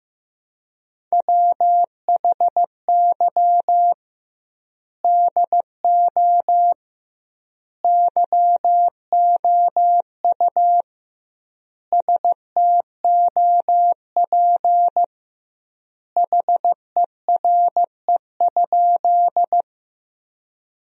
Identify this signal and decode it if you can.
Morse code, heard as a keyed tone, about 16 wpm, 700 Hz. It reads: WHY DO YOU STOP HERE?
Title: 2x Word-Spacing - Sentences from Top 300 Words - 15wpm